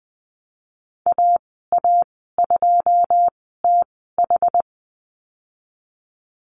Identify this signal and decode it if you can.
Morse code, heard as a keyed tone, about 19 wpm, spaced as for 14 wpm, 700 Hz. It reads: AA2TH
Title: 167 Lesson 27g - Number 7 - Call Signs 20wpm